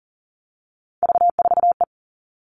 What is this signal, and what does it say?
Morse code, heard as a keyed tone, about 39 wpm, 700 Hz. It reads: V4E
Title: Three Challenging Letter-Number Combinations 40wpm